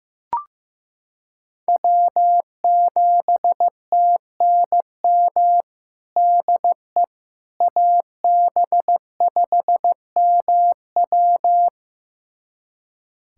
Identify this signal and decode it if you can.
Morse code, heard as a keyed tone, about 15 wpm, 700 Hz. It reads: W7TNM DE AB5MW
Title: USA Callsign DE Callsign - 5 Characters - 15wpm